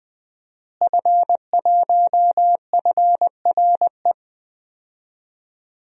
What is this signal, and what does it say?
Morse code, heard as a keyed tone, about 20 wpm, 700 Hz. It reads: F1FRE
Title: DXCC-100 Call Signs - 5 Characters - Repeated 3x - 20wpm